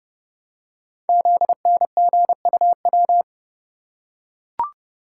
Morse code, heard as a keyed tone, about 30 wpm, 700 Hz. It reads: ZNGUW
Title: Code Groups - 5 Letters - 30wpm